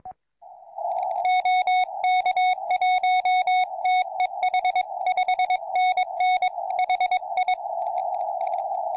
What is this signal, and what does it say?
Morse code, heard as a keyed tone, about 22 wpm, 2200 Hz, with noise. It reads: OK1TEH 5NN HI
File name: tropo / oh1nd